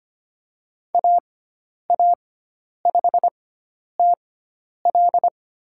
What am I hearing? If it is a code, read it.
Morse code, heard as a keyed tone, about 25 wpm, spaced as for 10 wpm, 700 Hz. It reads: AA5TL
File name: Lesson 16g - Letter c - Call Signs 25wpm